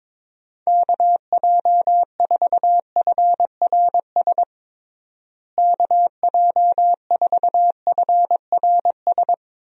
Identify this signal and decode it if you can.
Morse code, heard as a keyed tone, about 22 wpm, 700 Hz. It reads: KJ4FRS KJ4FRS
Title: USA Call Signs - 6 Characters - Repeated 3x - 22wpm